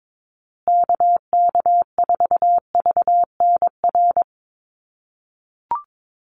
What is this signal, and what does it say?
Morse code, heard as a keyed tone, about 22 wpm, 700 Hz. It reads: KK4VNR
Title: USA Call Signs Repeated 2x 22wpm